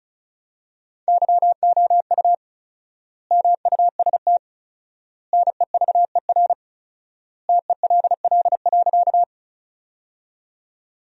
Morse code, heard as a keyed tone, about 34 wpm, 700 Hz. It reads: YOU MUST NEVER TELL.
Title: 4x Word-Spacing - Sentences from Top 300 Words - 35wpm